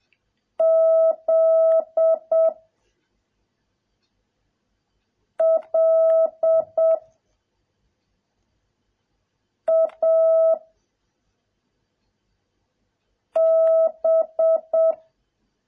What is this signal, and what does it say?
Morse code, heard as a keyed tone, about 7 wpm, 650 Hz. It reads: Z L A B